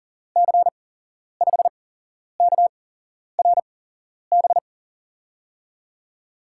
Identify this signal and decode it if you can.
Morse code, heard as a keyed tone, about 42 wpm, spaced as for 11 wpm, 700 Hz. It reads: C5KRB
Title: USA Call Signs - 6 Characters - 8x Character Spacing - 40wpm